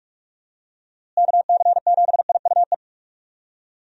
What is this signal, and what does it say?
Morse code, heard as a keyed tone, about 46 wpm, 700 Hz. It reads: KC7IUE
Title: USA Call Signs - 6 Characters - Repeated 3x - 45wpm